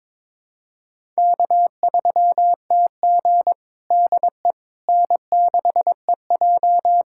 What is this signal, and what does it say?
Morse code, heard as a keyed tone, about 22 wpm, 700 Hz. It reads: K3TG DE N6EJ
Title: USA Callsign DE Callsign - 4 Characters - 22wpm